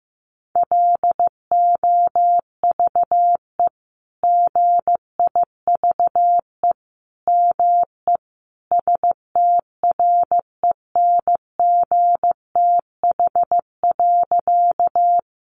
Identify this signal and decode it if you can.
Morse code, heard as a keyed tone, about 15 wpm, 700 Hz. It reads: LOVE GIVE ME STRENGTH.
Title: Sentences from Top 1300 Words 15wpm